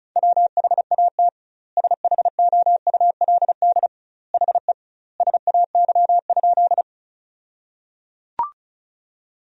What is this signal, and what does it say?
Morse code, heard as a keyed tone, about 36 wpm, 700 Hz. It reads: WHAT SHOULD HE SAY?
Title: 2x Word-Spacing - Sentences from Top 300 Words - 35wpm